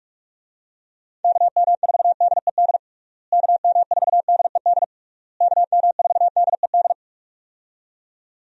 Morse code, heard as a keyed tone, about 45 wpm, 700 Hz. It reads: KM4DED KM4DED KM4DED
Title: USA Call Signs - 6 Characters - Repeated 3x - 45wpm